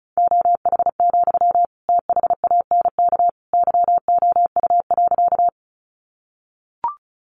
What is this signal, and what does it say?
Morse code, heard as a keyed tone, about 34 wpm, 700 Hz. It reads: OH, THANK YOU.
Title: Sentences from Top 600 Words 35wpm